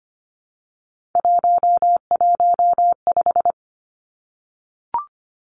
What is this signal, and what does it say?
Morse code, heard as a keyed tone, about 25 wpm, 700 Hz. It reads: 115
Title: Three Challenging Letter-Number Combinations 25wpm